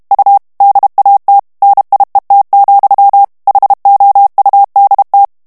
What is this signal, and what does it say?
Morse code, heard as a keyed tone, about 32 wpm, 800 Hz. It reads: U DAT NIET, HOUDT
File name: HOLLANDAIS texte clair 2 vitesse 30 mots minute